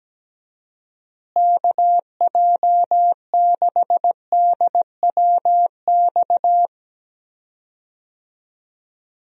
Morse code, heard as a keyed tone, about 17 wpm, 700 Hz. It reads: KJ6DWX
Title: USA Call Signs - 6 Characters - Repeated 3x - 17wpm